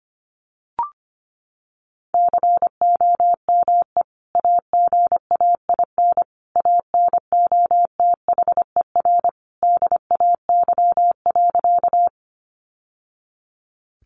Morse code, heard as a keyed tone, about 25 wpm, 700 Hz. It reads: COME AGAIN ANOTHER DAY.
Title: Sentences from Top 600 Words 25wpm